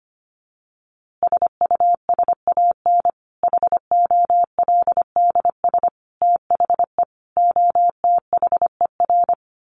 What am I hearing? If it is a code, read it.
Morse code, heard as a keyed tone, about 25 wpm, 700 Hz. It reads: SUSAN HOLDS THE OTHER
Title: Easy Phrases 25wpm